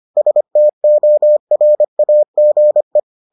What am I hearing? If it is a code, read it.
Morse code, heard as a keyed tone, about 25 wpm, 600 Hz. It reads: STORAGE